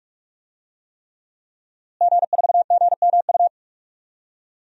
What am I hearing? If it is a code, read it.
Morse code, heard as a keyed tone, about 42 wpm, 700 Hz. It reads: G4GMU
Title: DXCC-100 Call Signs - 5 Characters - Repeated 3x - 45wpm